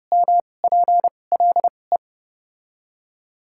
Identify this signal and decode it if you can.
Morse code, heard as a keyed tone, about 30 wpm, spaced as for 22 wpm, 700 Hz. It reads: MPLE
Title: 2x Character Spacing - Top 300 Words - 30wpm